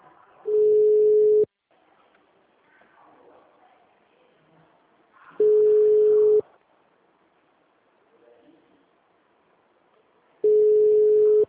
Ringback tone (Europe, 425 Hz).